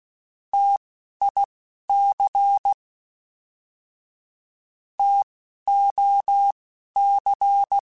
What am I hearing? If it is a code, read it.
Morse code, heard as a keyed tone, about 16 wpm, spaced as for 12 wpm, 800 Hz. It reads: TIC TOC